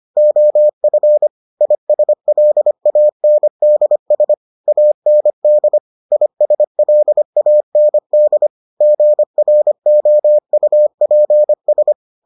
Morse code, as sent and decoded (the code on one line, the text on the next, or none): --- ..-. / .. ... .-.. .- -. -.. ... / .- -. -.. / .. ... .-.. .- -. -.. / --. .-. --- ..- .--. ...
OF ISLANDS AND ISLAND GROUPS